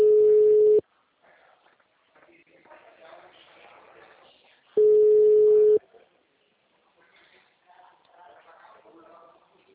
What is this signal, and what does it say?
Ringback tone (Europe, 425 Hz), with speech.